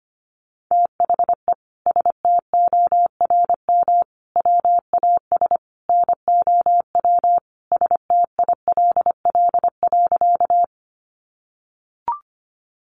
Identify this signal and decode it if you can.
Morse code, heard as a keyed tone, about 25 wpm, 700 Hz. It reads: THE STORM WAS NOW STILL.
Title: Sentences from Top 1500 Words 25wpm